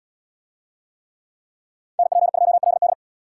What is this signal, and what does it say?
Morse code, heard as a keyed tone, about 69 wpm, 700 Hz. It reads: NP2LD